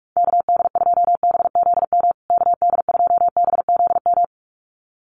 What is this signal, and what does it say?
Morse code, heard as a keyed tone, about 44 wpm, 700 Hz. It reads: KD2BZM KD2BZM
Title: USA Call Signs Repeated 2x 45wpm